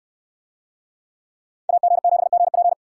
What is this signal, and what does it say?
Morse code, heard as a keyed tone, about 68 wpm, 700 Hz. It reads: AG7GP